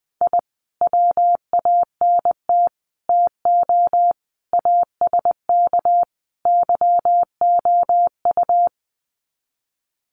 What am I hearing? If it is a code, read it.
Morse code, heard as a keyed tone, about 20 wpm, 700 Hz. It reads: I WANT TO ASK YOU
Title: Sets of 5 Words 20wpm